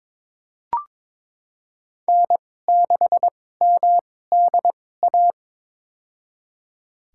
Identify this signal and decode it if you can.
Morse code, heard as a keyed tone, about 22 wpm, spaced as for 16 wpm, 700 Hz. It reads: N6MDA
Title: USA Call Signs - 5 Characters - 2x Character Spacing - 22wpm